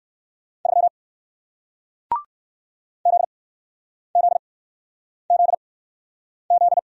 Morse code, heard as a keyed tone, about 57 wpm, spaced as for 9 wpm, 700 Hz. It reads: 3 ZZZZ